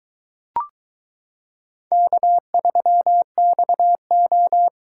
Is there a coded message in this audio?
Yes: Morse code from 1.9 to 4.7 s, heard as a keyed tone, about 22 wpm, 700 Hz. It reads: K3XO